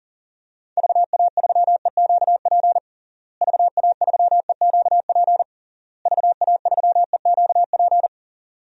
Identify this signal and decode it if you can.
Morse code, heard as a keyed tone, about 40 wpm, 700 Hz. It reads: VA3EQP VA3EQP VA3EQP
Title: DXCC-100 Call Signs - 6 Characters - Repeated 3x - 40wpm